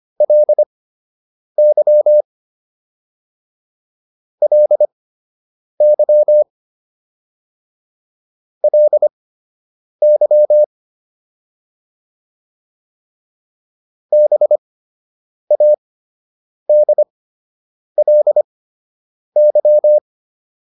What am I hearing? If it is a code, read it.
Morse code, heard as a keyed tone, about 25 wpm, spaced as for 8 wpm, 600 Hz. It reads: LY LY LY BADLY